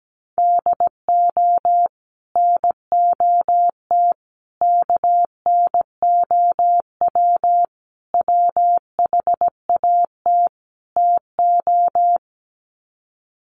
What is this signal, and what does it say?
Morse code, heard as a keyed tone, about 17 wpm, 700 Hz. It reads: DO NOT KNOW WHAT TO
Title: Sets of 5 Words 17wpm